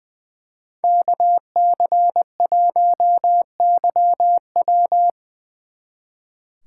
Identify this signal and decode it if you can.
Morse code, heard as a keyed tone, about 24 wpm, 700 Hz. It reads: KC1YW
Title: Speed Racing - USA Call Signs 20wpm